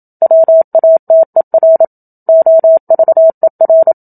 Morse code, heard as a keyed tone, about 27 wpm, 650 Hz. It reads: WATER OVER
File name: Book Two / 01-Chapter 1 - 27 WPM